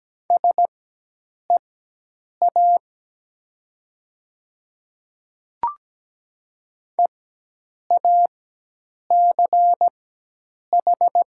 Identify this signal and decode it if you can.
Morse code, heard as a keyed tone, about 17 wpm, spaced as for 8 wpm, 700 Hz. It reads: SEA EACH
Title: 4x Character Spacing - Top 500 Words - 17wpm